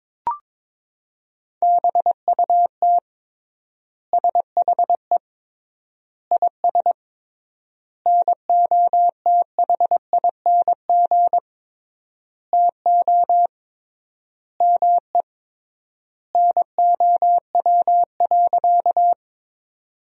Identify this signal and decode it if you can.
Morse code, heard as a keyed tone, about 22 wpm, 700 Hz. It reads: BUT SHE IS NOTHING TO ME NOW.
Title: 3x Word-Spacing - Sentences from Top 500 Words - 22wpm